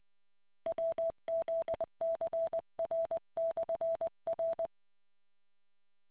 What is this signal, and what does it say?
Morse code, heard as a keyed tone, about 20 wpm, 660 Hz. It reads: WZCR/R